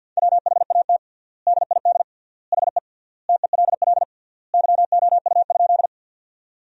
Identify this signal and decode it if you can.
Morse code, heard as a keyed tone, about 52 wpm, 700 Hz. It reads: WHAT DID HE TELL YOU?
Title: 3x Word-Spacing - Sentences from Top 200 Words - 50wpm